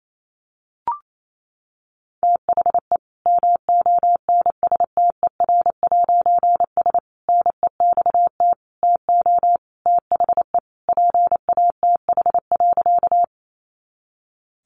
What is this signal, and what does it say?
Morse code, heard as a keyed tone, about 28 wpm, 700 Hz. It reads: THE MONSTER'S NEXT TO THE PATH.
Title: A Sound Of Thunder 28wpm